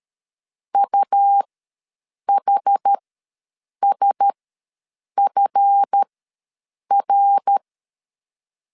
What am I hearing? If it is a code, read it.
Morse code, heard as a keyed tone, about 13 wpm, spaced as for 7 wpm, 800 Hz. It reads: UHSFR
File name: LEZIONI / 10 SIGNS / 10C DOT ws 12wpm 3548